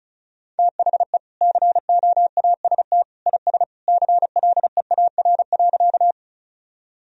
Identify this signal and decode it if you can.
Morse code, heard as a keyed tone, about 34 wpm, 700 Hz. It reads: THE COAST IS CLEAR.
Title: Sentences from Top 1000 Words - Review Part 4 - 35wpm